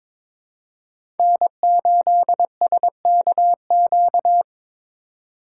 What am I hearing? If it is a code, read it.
Morse code, heard as a keyed tone, about 23 wpm, 700 Hz. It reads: N8SKQ
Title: USA Call Signs - 5 Characters - 3x Character Spacing - 22wpm